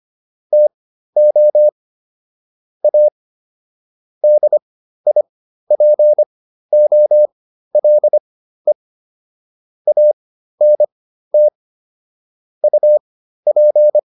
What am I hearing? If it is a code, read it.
Morse code, heard as a keyed tone, about 25 wpm, spaced as for 13 wpm, 600 Hz. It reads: TO A DIPOLE ANT UP